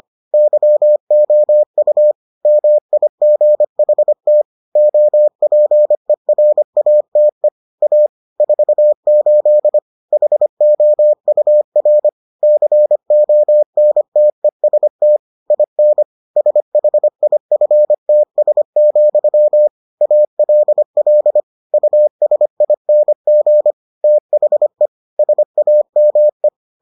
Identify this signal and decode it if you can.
Morse code, heard as a keyed tone, about 25 wpm, 600 Hz. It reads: YOU MIGHT OPERATE A 48 HOUR CONTEST IN SHIFTS, ALL USING THE SAME